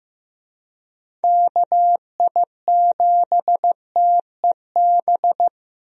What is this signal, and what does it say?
Morse code, heard as a keyed tone, about 15 wpm, 700 Hz. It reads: KI7TEB
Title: USA Call Signs - 6 Characters - Repeated 3x - 15wpm